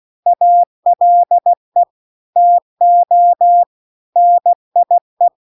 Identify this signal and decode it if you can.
Morse code, heard as a keyed tone, about 16 wpm, 700 Hz. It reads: ALE TO NIE